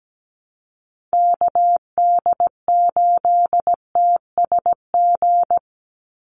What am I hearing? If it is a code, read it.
Morse code, heard as a keyed tone, about 17 wpm, 700 Hz. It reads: KD8TSG
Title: USA Call Signs 17wpm